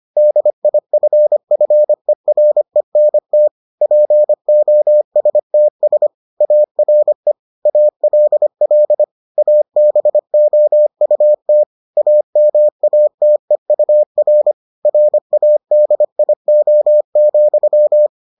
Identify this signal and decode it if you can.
Morse code, heard as a keyed tone, about 25 wpm, 600 Hz. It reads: DIFFERENT POSTS ARE ALL ABOUT AMATEUR RADIO,